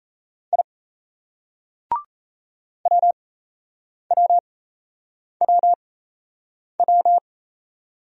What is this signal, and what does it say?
Morse code, heard as a keyed tone, about 38 wpm, spaced as for 8 wpm, 700 Hz. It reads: I WWWW